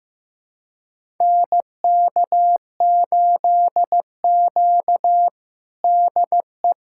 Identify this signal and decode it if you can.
Morse code, heard as a keyed tone, about 15 wpm, 700 Hz. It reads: NK8Q DE
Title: USA Callsign DE Callsign 15wpm